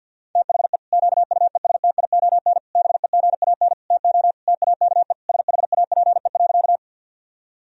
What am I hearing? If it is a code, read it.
Morse code, heard as a keyed tone, about 50 wpm, 700 Hz. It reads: THE QUESTION BEGAN TO TAKE SHAPE.
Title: Sentences from Top 1000 Words - Review Part 3 - 50wpm